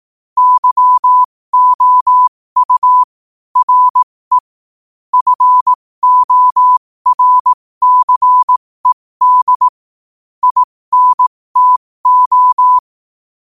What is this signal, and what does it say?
Morse code, heard as a keyed tone, about 18 wpm, spaced as for 15 wpm, 1000 Hz. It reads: YOU RE FORCED INTO
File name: qotdfiles / 15wpm